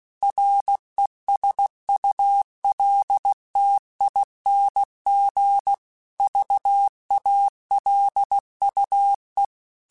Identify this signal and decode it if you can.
Morse code, heard as a keyed tone, about 16 wpm, 800 Hz. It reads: RESULTING VALUE
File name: Technical Subjects / Technical Subjects-mp3s / XC @15